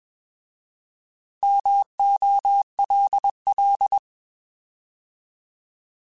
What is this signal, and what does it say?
Morse code, heard as a keyed tone, about 21 wpm, 800 Hz. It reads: MOLL
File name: Alphabetical / Alphabetical mp3s / Ms @20